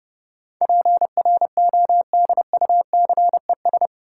Morse code, heard as a keyed tone, about 30 wpm, 700 Hz. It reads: PRODUCES
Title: Sentences from Top 1000 Words - Review Part 6 - New Word Drill - 30wpm